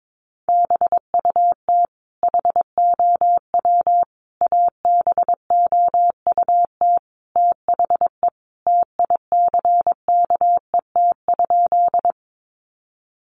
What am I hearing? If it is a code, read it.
Morse code, heard as a keyed tone, about 22 wpm, 700 Hz. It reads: BUT HOW ABOUT THE TICKET?
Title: Sentences from Top 1400 Words 22wpm